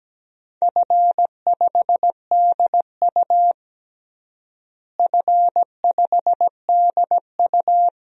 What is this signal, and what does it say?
Morse code, heard as a keyed tone, about 17 wpm, 700 Hz. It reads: F5DU F5DU
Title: DXCC-100 Call Signs - 4 Characters - Repeated 2x - 17wpm